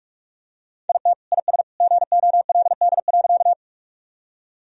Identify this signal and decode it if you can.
Morse code, heard as a keyed tone, about 41 wpm, 700 Hz. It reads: IT IS GOLD.
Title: Sentences from Top 1000 Words - Review Part 3 - 45wpm